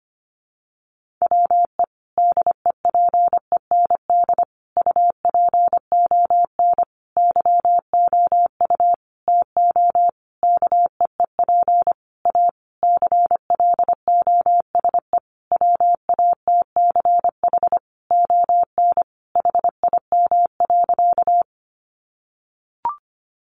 Morse code, heard as a keyed tone, about 25 wpm, 700 Hz. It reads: WE DEPEND UPON YOU TO KEEP A CLOSE WATCH ON HIM.
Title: Sentences from Top 1300 Words 25wpm